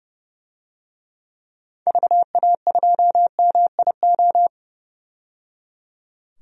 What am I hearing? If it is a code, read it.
Morse code, heard as a keyed tone, about 30 wpm, 700 Hz. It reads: VA2MIO
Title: DXCC-100 Call Signs - 6 Characters - 30wpm